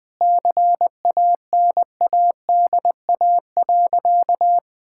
Morse code, heard as a keyed tone, about 21 wpm, 700 Hz. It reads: CANADA.
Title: Sentences with 5-Word Phrases - 20wpm